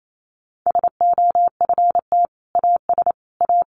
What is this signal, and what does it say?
Morse code, heard as a keyed tone, about 28 wpm, 700 Hz. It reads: SOFT AS A
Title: Sentences from Top 1500 Words 28wpm